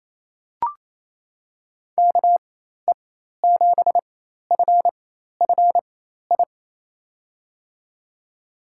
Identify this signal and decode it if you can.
Morse code, heard as a keyed tone, about 28 wpm, spaced as for 13 wpm, 700 Hz. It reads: KE7FFI
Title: USA Call Signs - 6 Characters - 4x Character Spacing - 28wpm